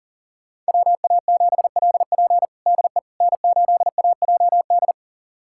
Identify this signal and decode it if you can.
Morse code, heard as a keyed tone, about 40 wpm, 700 Hz. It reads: WA7LP DE N8AJD
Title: USA Callsign DE Callsign - 5 Characters - 40wpm